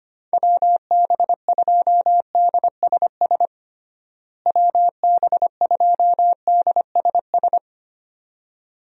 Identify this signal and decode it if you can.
Morse code, heard as a keyed tone, about 25 wpm, 700 Hz. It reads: WB2DSS WB2DSS
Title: USA Call Signs - 6 Characters - Repeated 2x - 25wpm